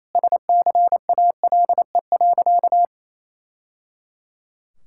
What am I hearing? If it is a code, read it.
Morse code, heard as a keyed tone, about 28 wpm, 700 Hz. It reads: SCALE.
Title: Sentences from Top 1000 Words - Review Part 5 - 28wpm